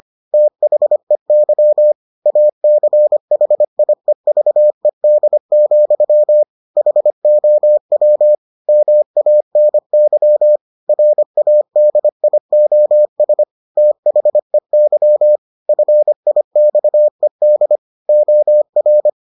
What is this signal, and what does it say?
Morse code, heard as a keyed tone, about 25 wpm, 600 Hz. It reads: THEY ACHIEVED, HOW MANY RADIOS THEY FIXED OR